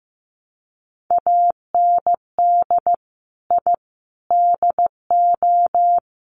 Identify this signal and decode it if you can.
Morse code, heard as a keyed tone, about 15 wpm, 700 Hz. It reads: AND I DO
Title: Sets of 5 Words 15wpm